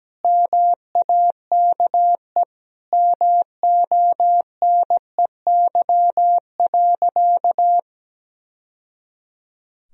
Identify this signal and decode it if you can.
Morse code, heard as a keyed tone, about 17 wpm, 700 Hz. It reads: MAKE MONEY.
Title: Sentences from Top 1000 Words - Review Part 4 - 17wpm